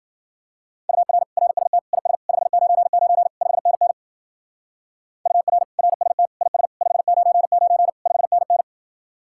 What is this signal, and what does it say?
Morse code, heard as a keyed tone, about 52 wpm, 700 Hz. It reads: UR RST IS 599 5NN UR RST IS 599 5NN